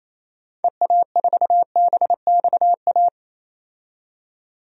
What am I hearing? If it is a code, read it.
Morse code, heard as a keyed tone, about 28 wpm, 700 Hz. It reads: EA4BXA
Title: DXCC-100 Call Signs - 6 Characters - 28wpm